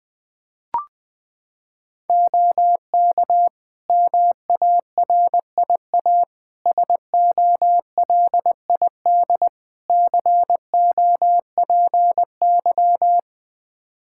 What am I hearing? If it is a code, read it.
Morse code, heard as a keyed tone, about 20 wpm, 700 Hz. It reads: OK MARIA SOLID COPY
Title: Practice QSO - Exchange 3 - More Info - 20wpm